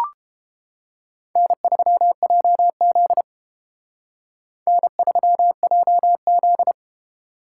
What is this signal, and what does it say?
Morse code, heard as a keyed tone, about 29 wpm, 700 Hz. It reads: N3JZ N3JZ